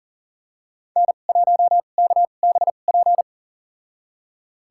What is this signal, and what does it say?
Morse code, heard as a keyed tone, about 40 wpm, spaced as for 29 wpm, 700 Hz. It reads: N1KBP
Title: USA Call Signs - 2x Character Spacing - 40wpm